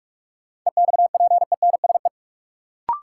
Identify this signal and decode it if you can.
Morse code, heard as a keyed tone, about 46 wpm, 700 Hz. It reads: EXPENSE
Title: Sets of 4 Words - New Word Drill - 45wpm